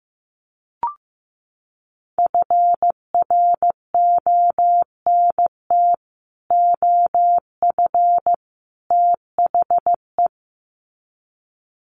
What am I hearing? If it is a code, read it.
Morse code, heard as a keyed tone, about 15 wpm, 700 Hz. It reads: FRONT OF THE
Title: Sets of 3 Words 15wpm